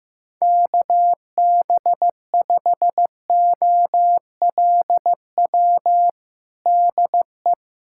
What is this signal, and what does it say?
Morse code, heard as a keyed tone, about 16 wpm, 700 Hz. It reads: KB5OLW DE
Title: USA Callsign DE Callsign 15wpm